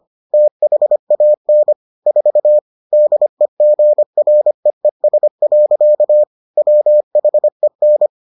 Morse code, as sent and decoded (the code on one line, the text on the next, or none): - .... .- -. / ....- / -.. --. .-. . ... .-.-.- / .-- .... -.
THAN 4 DEGREES. WHEN